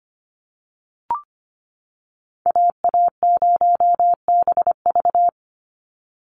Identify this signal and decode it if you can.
Morse code, heard as a keyed tone, about 25 wpm, 690 Hz. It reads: AA0BV
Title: NA SOTA Activator Callsigns 25wpm